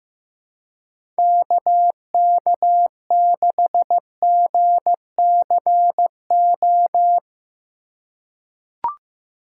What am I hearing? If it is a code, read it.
Morse code, heard as a keyed tone, about 15 wpm, 700 Hz. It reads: KK6GCO
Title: USA Call Signs - 6 Characters - 3x Character Spacing - 15wpm